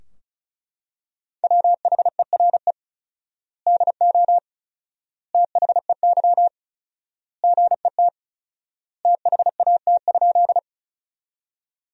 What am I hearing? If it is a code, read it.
Morse code, heard as a keyed tone, about 36 wpm, 700 Hz. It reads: WHERE DO THEY GET THAT?
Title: 4x Word-Spacing - Sentences from Top 200 Words - 35wpm